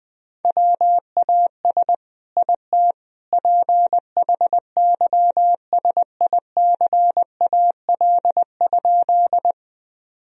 Morse code, heard as a keyed tone, about 19 wpm, 700 Hz. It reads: WAS IT PHYSICAL?
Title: Sentences from Top 1000 Words - Review Part 4 - 20wpm